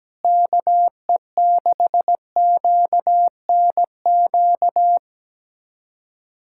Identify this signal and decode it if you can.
Morse code, heard as a keyed tone, about 17 wpm, 700 Hz. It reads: KE6QNQ
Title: USA Callsign DE Callsign - 6 Characters - 17wpm